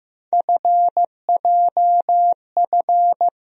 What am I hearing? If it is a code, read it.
Morse code, heard as a keyed tone, about 15 wpm, 700 Hz. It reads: FJF
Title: USA Call Sign Postfixes - 3 Letters 15wpm